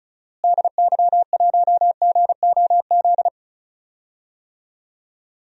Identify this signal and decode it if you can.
Morse code, heard as a keyed tone, about 36 wpm, 700 Hz. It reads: DY1GOZ
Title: DXCC-100 Call Signs - 6 Characters - Repeated 2x - 35wpm